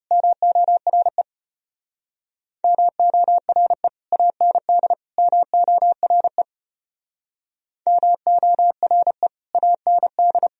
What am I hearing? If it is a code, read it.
Morse code, heard as a keyed tone, about 31 wpm, 700 Hz. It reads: MORE MORE AND MORE MORE AND
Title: Speed-Racing - Sets of 3 Words 25wpm